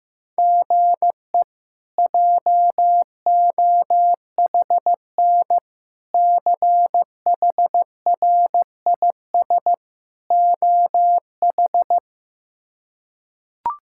Morse code, heard as a keyed tone, about 15 wpm, 700 Hz. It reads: GE JOHN CHRIS OH
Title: K1USN SST QSO Practice - 15wpm